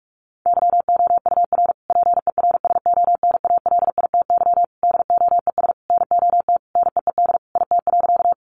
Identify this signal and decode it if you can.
Morse code, heard as a keyed tone, about 44 wpm, 700 Hz. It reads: YOUR PERSONALITY DOES NOT NEED IT.
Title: Sentences from Top 1400 Words 45wpm